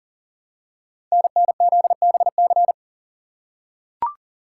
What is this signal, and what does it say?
Morse code, heard as a keyed tone, about 39 wpm, 700 Hz. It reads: NNZBC
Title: Code Groups - 5 Letters - 40wpm